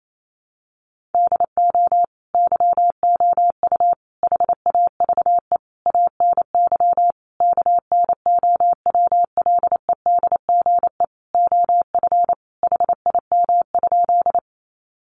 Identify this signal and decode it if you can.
Morse code, heard as a keyed tone, about 28 wpm, 700 Hz. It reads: DO YOU HAVE ANY KNOWLEDGE OF HIM?
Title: Sentences from Top 600 Words 28wpm